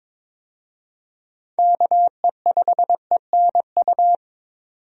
Morse code, heard as a keyed tone, about 22 wpm, 700 Hz. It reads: KE5ENU
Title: USA Call Signs - 6 Characters - 22wpm